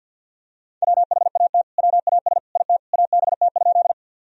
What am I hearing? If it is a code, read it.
Morse code, heard as a keyed tone, about 50 wpm, 700 Hz. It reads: WHAT WAS IT ABT?
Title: Sentences from Top 100 Words and Abbreviations 50wpm